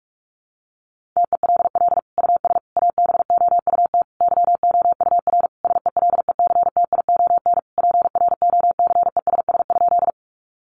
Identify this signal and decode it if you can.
Morse code, heard as a keyed tone, about 44 wpm, 700 Hz. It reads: TELL US ABOUT YOUR SELECTION PROCESS?
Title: Sentences from Top 1400 Words 45wpm